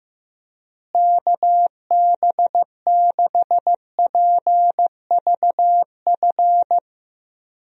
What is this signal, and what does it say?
Morse code, heard as a keyed tone, about 15 wpm, 700 Hz. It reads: KB6PVF
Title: USA Call Signs - 6 Characters - Repeated 2x - 15wpm